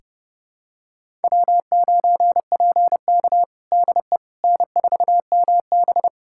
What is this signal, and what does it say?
Morse code, heard as a keyed tone, about 30 wpm, 700 Hz. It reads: W9PK DE N4MB
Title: USA Callsign DE Callsign - 4 Characters - 30wpm